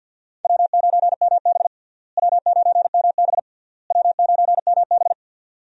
Morse code, heard as a keyed tone, about 46 wpm, 700 Hz. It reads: W9MB W9MB W9MB
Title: USA Call Signs - 4 Characters - Repeated 3x - 50wpm